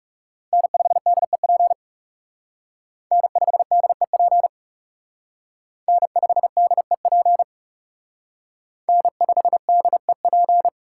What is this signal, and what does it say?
Morse code, heard as a keyed tone, about 37 wpm, 700 Hz. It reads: N5DEP N5DEP N5DEP N5DEP
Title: Speed Racing - USA 1x3 Call Signs 30wpm